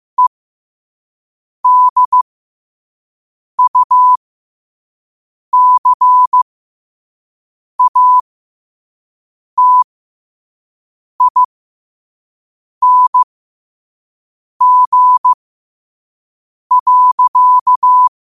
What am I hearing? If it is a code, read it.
Morse code, heard as a keyed tone, about 15 wpm, spaced as for 5 wpm, 1000 Hz. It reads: EDUCATING.
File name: qotdfiles / 5wpm